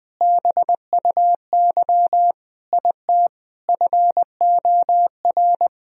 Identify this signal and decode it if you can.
Morse code, heard as a keyed tone, about 20 wpm, 700 Hz. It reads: BUY IT FOR
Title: Sentences from Top 1000 Words - Review Part 4 - 20wpm